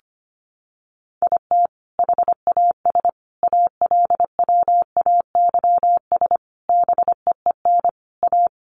Morse code, heard as a keyed tone, about 25 wpm, 700 Hz. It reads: IT HAS ALWAYS BEEN A
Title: Sentences from Top 1300 Words 25wpm